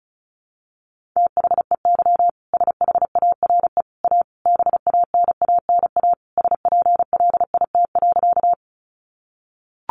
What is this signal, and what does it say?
Morse code, heard as a keyed tone, about 34 wpm, 700 Hz. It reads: THEY SHARE A BANANA SPLIT.
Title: Easy Phrases 35wpm